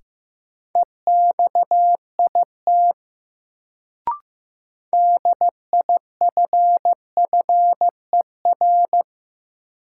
Morse code, heard as a keyed tone, about 15 wpm, 700 Hz. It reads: EXIT DIFFER
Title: Sentences from Top 1100 Words - New Word Drill Alt - 15wpm